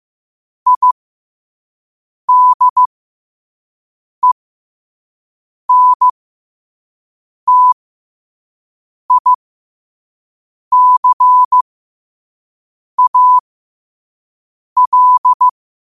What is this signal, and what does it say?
Morse code, heard as a keyed tone, about 15 wpm, spaced as for 5 wpm, 1000 Hz. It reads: IDENTICAL
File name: qotdfiles / 5wpm